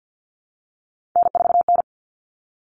Morse code, heard as a keyed tone, about 50 wpm, 700 Hz. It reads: N4N
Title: Common Cut Numbers - 3-Digit 50wpm